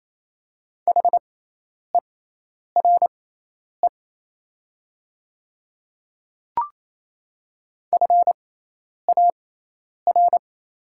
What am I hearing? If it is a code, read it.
Morse code, heard as a keyed tone, about 28 wpm, spaced as for 10 wpm, 700 Hz. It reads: HERE FAR
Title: 6x Character Spacing - Top 300 Words - 28wpm